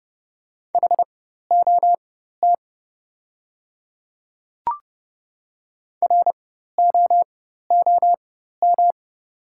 Morse code, heard as a keyed tone, about 30 wpm, spaced as for 14 wpm, 700 Hz. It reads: HOT ROOM
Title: 4x Character Spacing - Top 300 Words - 30wpm